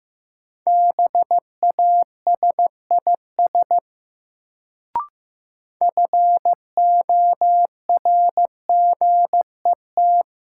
Morse code, heard as a keyed tone, about 15 wpm, 700 Hz. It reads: BASIS FORGET
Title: Sentences from Top 1200 Words - New Word Drill Alt - 15wpm